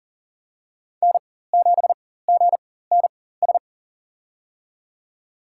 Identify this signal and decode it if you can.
Morse code, heard as a keyed tone, about 37 wpm, spaced as for 18 wpm, 700 Hz. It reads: N7GNS
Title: USA Call Signs - 4x Character Spacing - 40wpm